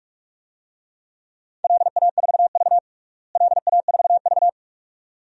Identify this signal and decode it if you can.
Morse code, heard as a keyed tone, about 44 wpm, 700 Hz. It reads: LA4V LA4V
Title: DXCC-100 Call Signs - 4 Characters - Repeated 2x - 45wpm